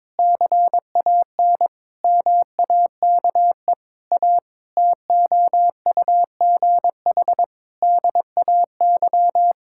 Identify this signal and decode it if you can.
Morse code, heard as a keyed tone, about 21 wpm, 700 Hz. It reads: CAN MAKE A TOUGH DAY
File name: Difficult Sentences with Hints - 22wpm